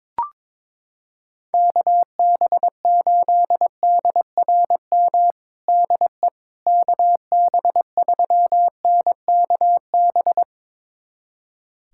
Morse code, heard as a keyed tone, about 21 wpm, 700 Hz. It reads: KB8DRM DE KB3NKB
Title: USA Callsign DE Callsign - 6 Characters - 22wpm